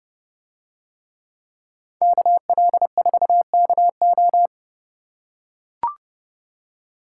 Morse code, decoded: KL4KO

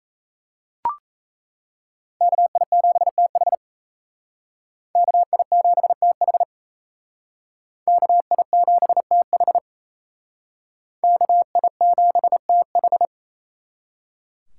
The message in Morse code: KI7TH KI7TH KI7TH KI7TH

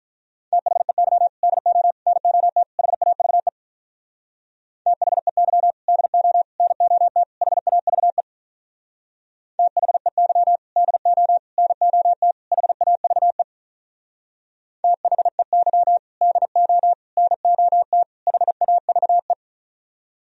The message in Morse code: THEY DO NOT HAVE THEY DO NOT HAVE THEY DO NOT HAVE THEY DO NOT HAVE

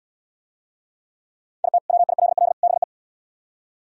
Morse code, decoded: IT WILL BE

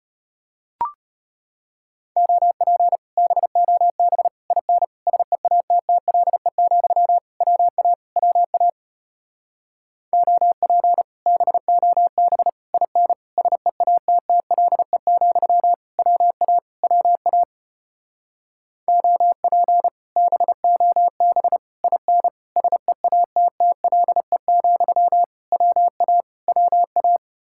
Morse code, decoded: OP BOB IN SEATTLE, WA WA OP BOB IN SEATTLE, WA WA OP BOB IN SEATTLE, WA WA

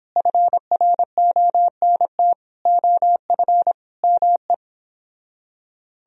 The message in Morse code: FRONT OF ME

